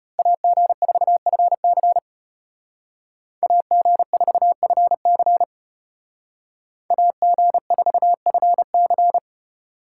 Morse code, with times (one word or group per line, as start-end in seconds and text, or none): 0.2-2.0 AG4FC
3.4-5.4 AG4FC
6.9-9.2 AG4FC